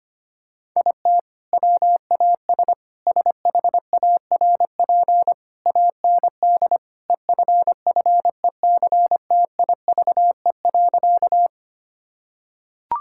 Morse code, heard as a keyed tone, about 25 wpm, 700 Hz. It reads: IT WAS SHARP AND EFFECTIVE.
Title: Sentences from Top 1600 Words 25wpm